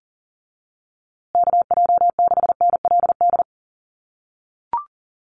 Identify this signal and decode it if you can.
Morse code, heard as a keyed tone, about 40 wpm, 700 Hz. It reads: KJ6NLD
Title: USA Call Signs Repeated 2x 40wpm